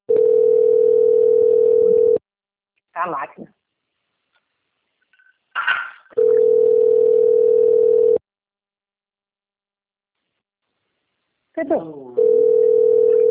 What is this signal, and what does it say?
Ringback tone (North America), 440 + 480 Hz, with speech.
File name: out- - - - - .39197